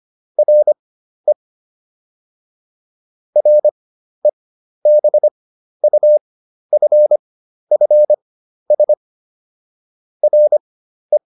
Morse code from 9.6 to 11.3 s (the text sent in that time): RE